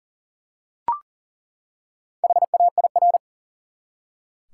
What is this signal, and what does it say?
Morse code, heard as a keyed tone, about 37 wpm, 700 Hz. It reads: HAIR
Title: Four Letter Words - 40wpm